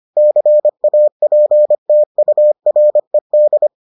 Morse code, heard as a keyed tone, about 25 wpm, 600 Hz. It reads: CAPTURED